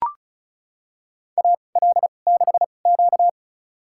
Morse code, heard as a keyed tone, about 36 wpm, spaced as for 25 wpm, 700 Hz. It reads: AL6Q
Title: USA Call Signs - 4 Characters - 2x Character Spacing - 35wpm